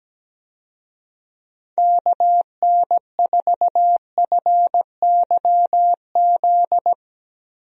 Morse code, heard as a keyed tone, about 17 wpm, 700 Hz. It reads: KN4FYZ
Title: USA Call Signs - 6 Characters - 17wpm